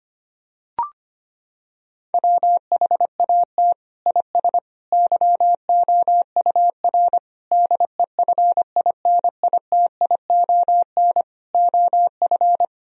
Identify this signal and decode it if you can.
Morse code, heard as a keyed tone, about 25 wpm, 700 Hz. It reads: WHAT IS YOUR DEFINITION OF
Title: Sentences from Top 1000 Words - Review Part 4 - 25wpm